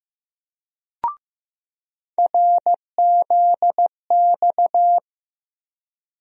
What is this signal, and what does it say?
Morse code, heard as a keyed tone, about 15 wpm, 700 Hz. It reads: RZX
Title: USA Call Sign Postfixes - 3 Letters 15wpm